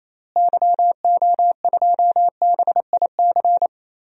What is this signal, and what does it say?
Morse code, heard as a keyed tone, about 28 wpm, 700 Hz. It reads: YO2BIC
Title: DXCC-100 Call Signs - 6 Characters - Repeated 3x - 28wpm